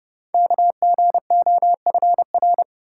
Morse code, heard as a keyed tone, about 28 wpm, 700 Hz. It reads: KGOFR
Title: Code Groups - 5 Letters - 30wpm